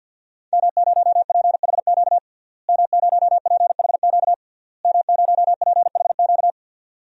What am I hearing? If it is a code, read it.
Morse code, heard as a keyed tone, about 50 wpm, 700 Hz. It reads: M0PHQ M0PHQ M0PHQ
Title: DXCC-100 Call Signs - 5 Characters - Repeated 3x - 50wpm